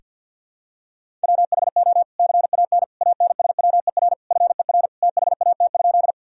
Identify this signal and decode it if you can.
Morse code, heard as a keyed tone, about 47 wpm, 700 Hz. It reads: WHO CAN ANSWER FER THAT?
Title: Sentences from Top 300 Words and Abbreviations 50wpm